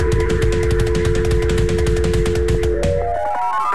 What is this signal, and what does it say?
Dial tone (North America), 350 + 440 Hz.